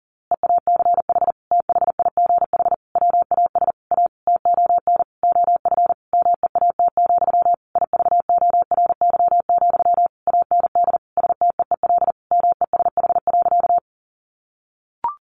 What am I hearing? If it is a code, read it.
Morse code, heard as a keyed tone, about 40 wpm, 700 Hz. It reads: EACH THIGH WAS A TON OF MEAT, IVORY, AND STEEL MESH.
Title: A Sound Of Thunder 40wpm